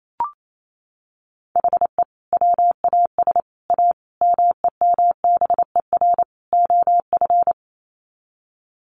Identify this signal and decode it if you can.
Morse code, heard as a keyed tone, about 28 wpm, 700 Hz. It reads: HE WAS A MEMBER OF
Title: Sets of 5 Words Encore2 28wpm